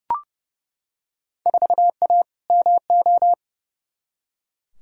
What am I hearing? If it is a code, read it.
Morse code, heard as a keyed tone, about 30 wpm, 700 Hz. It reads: 4A MO
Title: Field Day Exchanges - 30wpm